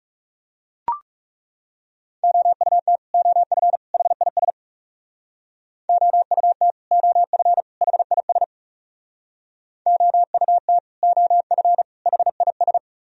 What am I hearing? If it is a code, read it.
Morse code, heard as a keyed tone, about 38 wpm, 700 Hz. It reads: OUT OF HIS OUT OF HIS OUT OF HIS